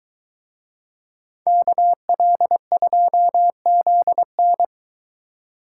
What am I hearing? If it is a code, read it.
Morse code, heard as a keyed tone, about 22 wpm, 700 Hz. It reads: KL2ZN